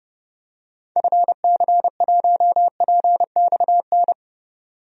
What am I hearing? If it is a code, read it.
Morse code, heard as a keyed tone, about 31 wpm, 700 Hz. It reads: FC1PXN
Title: DXCC-100 Call Signs - 6 Characters - 30wpm